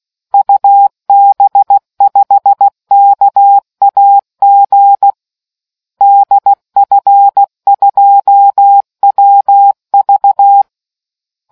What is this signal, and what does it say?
Morse code, heard as a keyed tone, about 16 wpm, 800 Hz. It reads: UB5KAG DF2WV